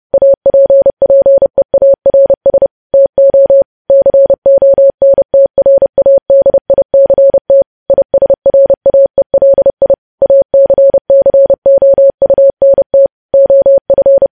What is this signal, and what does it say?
Morse code, heard as a keyed tone, about 30 wpm, 550 Hz. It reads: APPEARS TO CONTRADICT ISRAELI ACCOUNT OF
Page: news headlines in morse code at 30 wpm